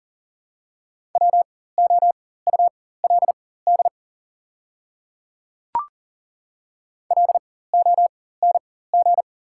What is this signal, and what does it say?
Morse code, heard as a keyed tone, about 40 wpm, spaced as for 19 wpm, 690 Hz. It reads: WOULD LONG
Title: 4x Character Spacing - Top 100 Words - 40wpm